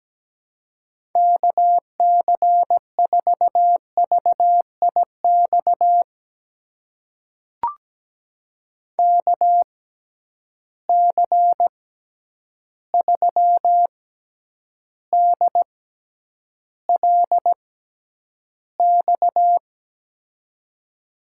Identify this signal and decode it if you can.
Morse code, heard as a keyed tone, about 17 wpm, spaced as for 9 wpm, 700 Hz. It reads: KC4VIX KC3DLX
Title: USA Call Signs - 6 Characters - 6x Character Spacing - 17wpm